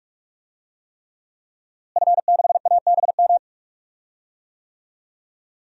Morse code, heard as a keyed tone, about 43 wpm, 700 Hz. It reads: F6ABM
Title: DXCC-100 Call Signs - 5 Characters - Repeated 3x - 45wpm